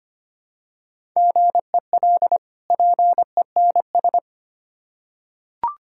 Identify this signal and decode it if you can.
Morse code, heard as a keyed tone, about 25 wpm, 700 Hz. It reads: GEL PENS
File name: practice / 25wpm / Difficult Words with Hints - 25wpm